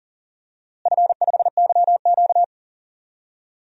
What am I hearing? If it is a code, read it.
Morse code, heard as a keyed tone, about 40 wpm, 700 Hz. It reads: F5YQ
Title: DXCC-100 Call Signs - 4 Characters - Repeated 3x - 40wpm